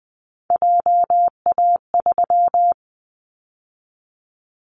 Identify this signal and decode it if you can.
Morse code, heard as a keyed tone, about 20 wpm, 700 Hz. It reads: JA3
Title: Callsign Prefixes for Top 50 DXCC - 20wpm